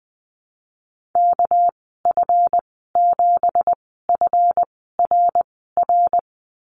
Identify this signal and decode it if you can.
Morse code, heard as a keyed tone, about 19 wpm, spaced as for 14 wpm, 700 Hz. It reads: KF7FRR
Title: 238 Lesson 38g - Letter q - Call Signs 20wpm